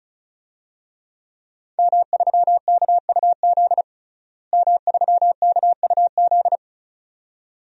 Morse code, heard as a keyed tone, about 36 wpm, 700 Hz. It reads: M3KUZ M3KUZ
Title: DXCC-100 Call Signs - 5 Characters - Repeated 2x - 35wpm